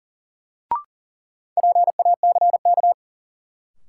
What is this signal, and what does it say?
Morse code, heard as a keyed tone, about 39 wpm, 700 Hz. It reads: PACK
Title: Sentences from Top 1300 Words - New Word Drill Alt - 40wpm